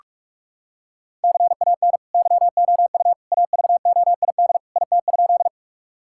Morse code, heard as a keyed tone, about 46 wpm, 700 Hz. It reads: CAN YOU AVOID IT?
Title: Sentences from Top 1000 Words - Review Part 6 - 45wpm